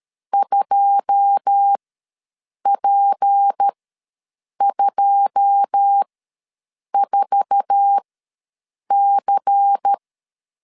Morse code, heard as a keyed tone, about 13 wpm, spaced as for 7 wpm, 800 Hz. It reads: 2P24C
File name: LEZIONI / 9 2 ws 12wpm 3406